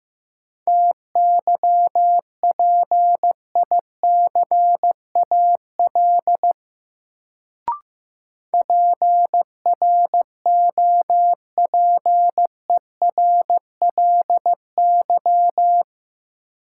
Morse code, heard as a keyed tone, about 15 wpm, 700 Hz. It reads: TYPICAL PROPERLY
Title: Sentences from Top 1100 Words - New Word Drill Alt - 15wpm